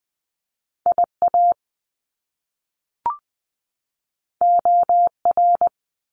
Morse code, heard as a keyed tone, about 20 wpm, 700 Hz. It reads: IA OR
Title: US States 20wpm